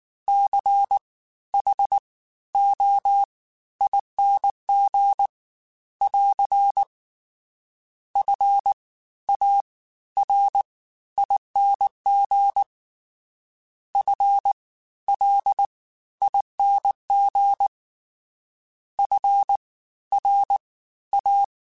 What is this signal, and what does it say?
Morse code, heard as a keyed tone, about 19 wpm, spaced as for 13 wpm, 800 Hz. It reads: CHOING <AR> FARING FLING FRA